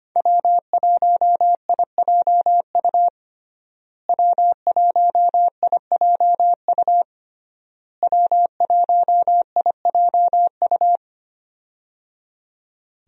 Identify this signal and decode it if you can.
Morse code, heard as a keyed tone, about 25 wpm, 700 Hz. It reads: W1IJU W1IJU W1IJU
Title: USA Call Signs - 5 Characters - Repeated 3x - 25wpm